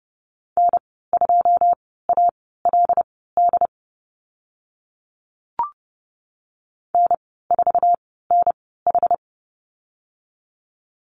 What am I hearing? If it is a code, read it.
Morse code, heard as a keyed tone, about 30 wpm, spaced as for 17 wpm, 700 Hz. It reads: N2ALD N4NH
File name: Lesson 14g - Number 2 - Call Signs 30wpm